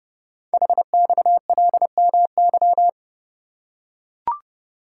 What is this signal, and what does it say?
Morse code, heard as a keyed tone, about 30 wpm, 700 Hz. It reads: HXLMY
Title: Code Groups - 5 Letters - 30wpm